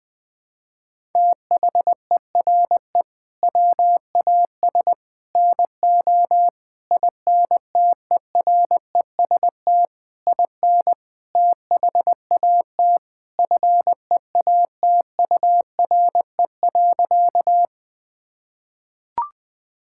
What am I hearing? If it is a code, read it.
Morse code, heard as a keyed tone, about 20 wpm, 700 Hz. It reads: THERE WAS NO INTEREST IN THAT FEATURE.
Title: Sentences from Top 1000 Words - Review Part 5 - 20wpm